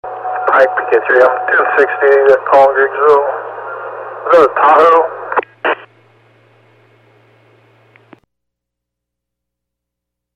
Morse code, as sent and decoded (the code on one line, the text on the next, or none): none